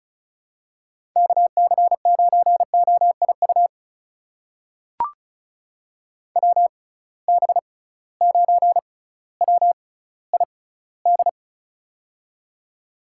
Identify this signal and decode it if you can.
Morse code, heard as a keyed tone, about 36 wpm, spaced as for 18 wpm, 700 Hz. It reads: KC9OIU WB9WID
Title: USA Call Signs - 6 Characters - 6x Character Spacing - 35wpm